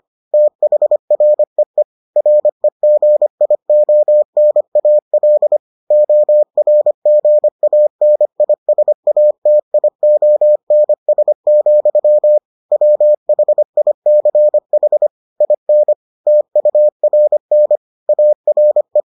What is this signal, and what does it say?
Morse code, heard as a keyed tone, about 25 wpm, 600 Hz. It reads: THREE REGIONAL ORGANISATIONS, WHICH IN TURN ARE